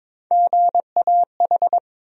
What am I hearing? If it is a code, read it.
Morse code, heard as a keyed tone, about 22 wpm, 700 Hz. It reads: GAH
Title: USA Call Sign Postfixes - 3 Letters 22wpm